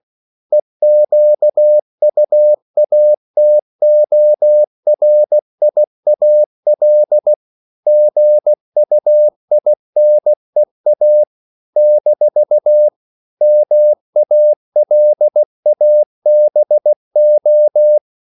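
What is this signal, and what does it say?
Morse code, heard as a keyed tone, about 16 wpm, 600 Hz. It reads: EQUATORIAL GUINEA - MALABO
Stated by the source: CW practice of national capitals at 16 words per minute, from ON6ZQ.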